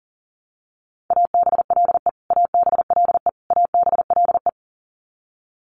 Morse code, heard as a keyed tone, about 40 wpm, 700 Hz. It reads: ABLE ABLE ABLE
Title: Top 500 Words Repeated 40wpm